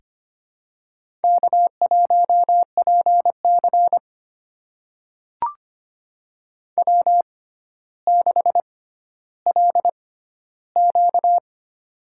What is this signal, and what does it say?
Morse code, heard as a keyed tone, about 25 wpm, spaced as for 13 wpm, 700 Hz. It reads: K1PC W6LQ
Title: USA Call Signs - 6x Character Spacing - 25wpm